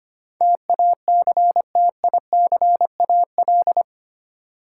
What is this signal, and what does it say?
Morse code, heard as a keyed tone, about 25 wpm, 690 Hz. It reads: TACTICAL
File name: 8 Letter Words - First 3 Letters Given - 25wpm